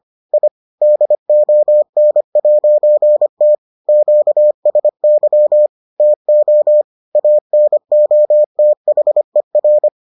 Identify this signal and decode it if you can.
Morse code, heard as a keyed tone, about 25 wpm, 600 Hz. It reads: I DON'T QSY TO ANOTHER